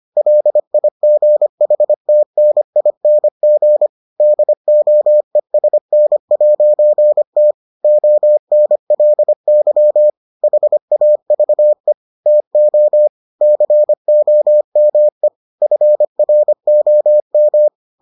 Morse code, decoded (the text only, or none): LIGHTNING DOESN'T ONLY HAVE TO COME FROM